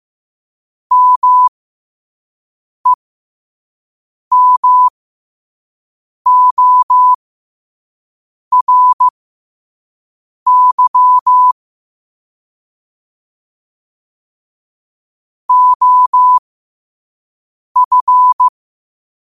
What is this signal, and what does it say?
Morse code, heard as a keyed tone, about 15 wpm, spaced as for 5 wpm, 1000 Hz. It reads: MEMORY OF
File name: qotdfiles / 5wpm